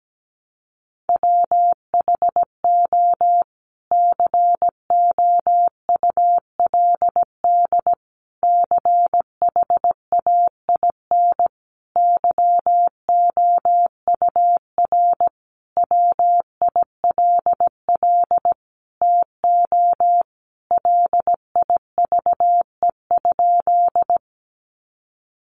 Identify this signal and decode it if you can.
Morse code, heard as a keyed tone, about 17 wpm, 700 Hz. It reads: WHO COULD CHAIN YOUR WILL TO LIVE?
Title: Sentences from Top 1500 Words 17wpm